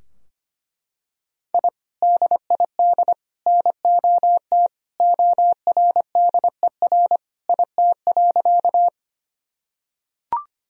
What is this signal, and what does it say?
Morse code, heard as a keyed tone, about 25 wpm, 700 Hz. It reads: I DID NOT ORDER IT.
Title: Sentences from Top 1000 Words - Review Part 2 - 25wpm